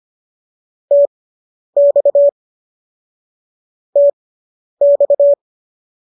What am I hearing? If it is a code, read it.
Morse code, heard as a keyed tone, about 25 wpm, spaced as for 10 wpm, 570 Hz. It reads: TX TX